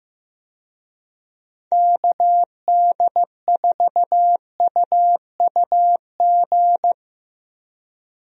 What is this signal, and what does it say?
Morse code, heard as a keyed tone, about 16 wpm, 700 Hz. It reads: KD4UUG
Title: USA Call Signs - 6 Characters - Repeated 2x - 15wpm